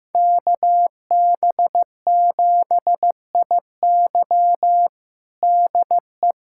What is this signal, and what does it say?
Morse code, heard as a keyed tone, about 15 wpm, 700 Hz. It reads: KB7IY DE
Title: USA Callsign DE Callsign - 5 Characters - 15wpm